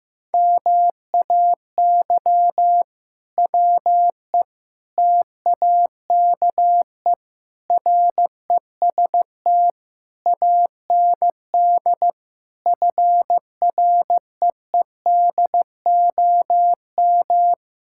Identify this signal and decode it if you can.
Morse code, heard as a keyed tone, about 15 wpm, 700 Hz. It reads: MAY WE TAKE REST AND FREEDOM
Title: Sentences from Top 1600 Words 15wpm